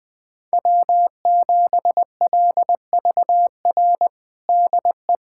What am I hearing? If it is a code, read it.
Morse code, heard as a keyed tone, about 20 wpm, 700 Hz. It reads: W7LVR DE
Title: USA Callsign DE Callsign - 5 Characters - 20wpm